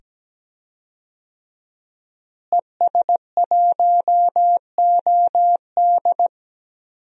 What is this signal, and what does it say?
Morse code, heard as a keyed tone, about 17 wpm, 700 Hz. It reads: ES1OD
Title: DXCC-100 Call Signs - 5 Characters - Repeated 2x - 17wpm